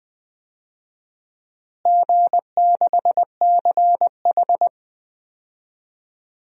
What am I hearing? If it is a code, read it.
Morse code, heard as a keyed tone, about 20 wpm, 700 Hz. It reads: G6CH
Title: DXCC-100 Call Signs - 4 Characters - Repeated 2x - 20wpm